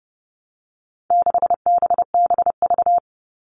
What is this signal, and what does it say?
Morse code, heard as a keyed tone, about 30 wpm, 700 Hz. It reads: 6BBV
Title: All Cut Numbers - 4-Digit 30wpm